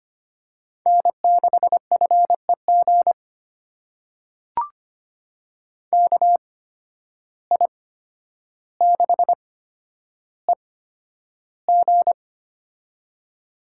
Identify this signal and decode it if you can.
Morse code, heard as a keyed tone, about 25 wpm, spaced as for 11 wpm, 700 Hz. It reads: N6FEG KI6EG
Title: USA Call Signs - 5 Characters - 8x Character Spacing - 25wpm